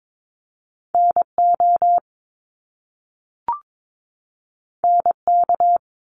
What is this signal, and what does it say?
Morse code, heard as a keyed tone, about 22 wpm, 700 Hz. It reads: NO NK
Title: Top Two Letter Combinations in Words 22wpm